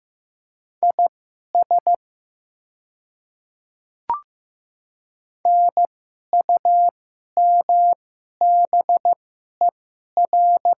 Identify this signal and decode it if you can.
Morse code, heard as a keyed tone, about 15 wpm, spaced as for 11 wpm, 700 Hz. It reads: IS NUMBER
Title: 2x Character Spacing - Top 100 Words - 15wpm